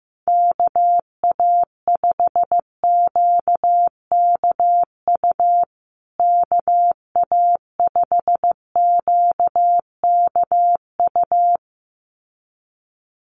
Morse code, heard as a keyed tone, about 15 wpm, 700 Hz. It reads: KA5QKU KA5QKU
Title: USA Call Signs Repeated 2x 15wpm